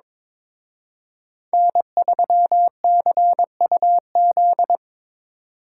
Morse code, heard as a keyed tone, about 25 wpm, 700 Hz. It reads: N3CUZ
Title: Speed Racing - USA Call Signs 22wpm